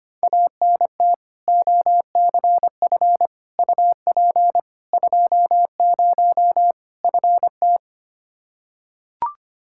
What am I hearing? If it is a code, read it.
Morse code, heard as a keyed tone, about 25 wpm, 700 Hz. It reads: ANT OCF UP 20 FT
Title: Practice QSO - Exchange 2 - Info - 25wpm